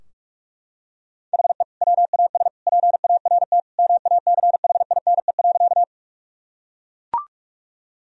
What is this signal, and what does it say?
Morse code, heard as a keyed tone, about 41 wpm, 700 Hz. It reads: HE WAS PART MACHINE.
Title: Sentences from Top 1000 Words - Review Part 2 - 45wpm